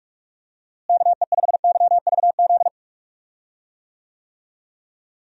Morse code, heard as a keyed tone, about 46 wpm, 700 Hz. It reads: KE5YVZ
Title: USA Call Signs - 6 Characters - Repeated 3x - 45wpm